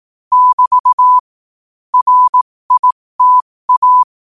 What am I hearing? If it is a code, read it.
Morse code, heard as a keyed tone, about 18 wpm, spaced as for 15 wpm, 1000 Hz. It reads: <BT> RITA
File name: qotdfiles / 15wpm